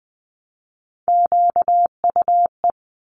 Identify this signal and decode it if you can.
Morse code, heard as a keyed tone, about 20 wpm, 700 Hz. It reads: QUE
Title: Multi-letter Phonemes 20wpm